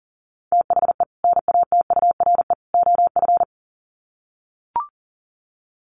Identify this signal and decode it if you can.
Morse code, heard as a keyed tone, about 40 wpm, 700 Hz. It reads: THE NATURE OF
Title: Sets of 3 Words 40wpm